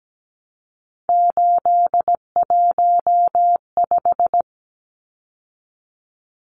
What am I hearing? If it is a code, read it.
Morse code, heard as a keyed tone, about 17 wpm, 700 Hz. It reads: 815
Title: Numbers - 3-Digit 17wpm